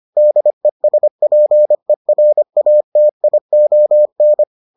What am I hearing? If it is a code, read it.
Morse code, heard as a keyed tone, about 25 wpm, 600 Hz. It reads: DESPERATION